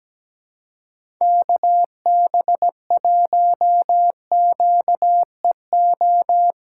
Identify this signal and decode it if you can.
Morse code, heard as a keyed tone, about 17 wpm, 700 Hz. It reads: KB1QEO
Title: USA Call Signs - 3x Character Spacing - 17wpm